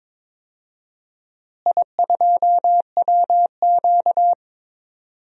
Morse code, heard as a keyed tone, about 22 wpm, 700 Hz. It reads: I2WQ
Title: DXCC-100 Call Signs - 4 Characters - Repeated 2x - 22wpm